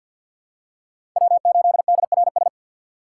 Morse code, heard as a keyed tone, about 51 wpm, 700 Hz. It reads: W8DRS
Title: USA Call Signs - 5 Characters - 4x Character Spacing - 50wpm